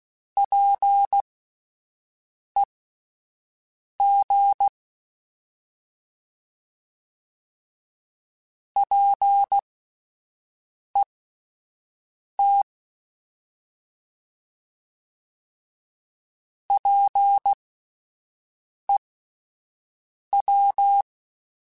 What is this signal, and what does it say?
Morse code, heard as a keyed tone, about 16 wpm, spaced as for 5 wpm, 800 Hz. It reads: PEG PET PEW